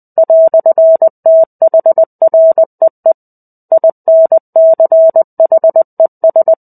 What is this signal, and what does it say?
Morse code, heard as a keyed tone, about 20 wpm, 650 Hz. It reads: "THREE INCHES
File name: 05-Chapter-05 20 WPM